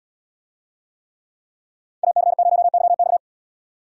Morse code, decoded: AC0YX

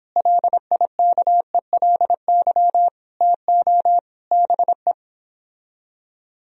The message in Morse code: LIKELY TO BE

